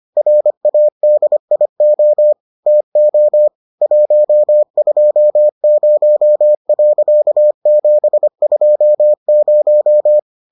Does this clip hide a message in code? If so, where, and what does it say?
Yes: Morse code from 0.2 to 10.2 s, heard as a keyed tone, about 25 wpm, 600 Hz. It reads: RADIO TO 120.720